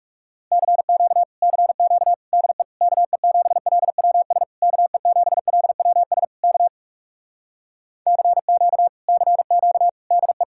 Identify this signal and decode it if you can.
Morse code, heard as a keyed tone, about 40 wpm, 700 Hz. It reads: CQ CQ DE KE7LWS KE7LWS K CQ CQ DE